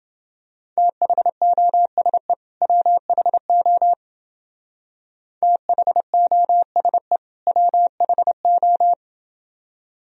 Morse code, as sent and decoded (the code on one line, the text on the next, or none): - .... --- ... / .-- .... --- / - .... --- ... / .-- .... ---
THOSE WHO THOSE WHO